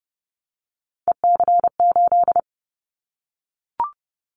Morse code, heard as a keyed tone, about 30 wpm, 700 Hz. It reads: EC8
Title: Callsign Prefixes for Top 100 DXCC - 30wpm